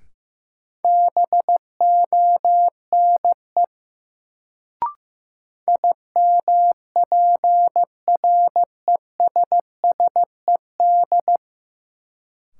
Morse code, heard as a keyed tone, about 15 wpm, 700 Hz. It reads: BONE IMPRESSED
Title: Sentences from Top 1600 Words - New Word Drill Alt - 15wpm